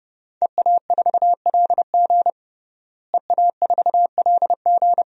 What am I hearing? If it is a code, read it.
Morse code, heard as a keyed tone, about 32 wpm, 700 Hz. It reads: EA4LG EA4LG
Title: DXCC-100 Call Signs - 5 Characters - Repeated 3x - 30wpm